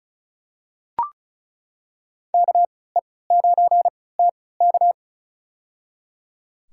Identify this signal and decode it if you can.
Morse code, heard as a keyed tone, about 34 wpm, spaced as for 20 wpm, 700 Hz. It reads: KE9TK
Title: USA Call Signs - 5 Characters - 3x Character Spacing - 35wpm